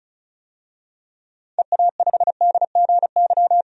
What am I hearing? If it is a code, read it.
Morse code, heard as a keyed tone, about 36 wpm, 700 Hz. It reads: EA5DGY
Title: DXCC-100 Call Signs - 6 Characters - 35wpm